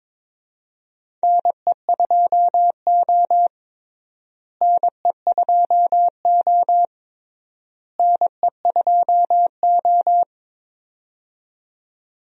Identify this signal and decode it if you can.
Morse code, heard as a keyed tone, about 22 wpm, 700 Hz. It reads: NE2O NE2O NE2O
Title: USA Call Signs - 4 Characters - Repeated 3x - 22wpm